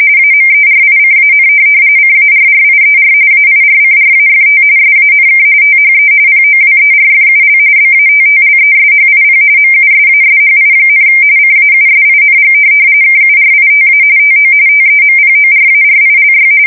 RTTY, about 45.45 baud. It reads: AMERICAN CARRIER AIRCRAFT SANK THE POWERFUL BATTLESHIP MUSASHI, CAUSING KURITA TO TEMPORARILY
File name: fsk170 / USS-Massachusetts